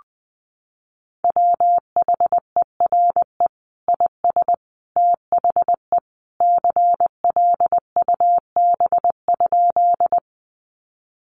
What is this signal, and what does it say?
Morse code, heard as a keyed tone, about 20 wpm, 700 Hz. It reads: WHERE IS THE CLUB?
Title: Sentences from Top 1200 Words 20wpm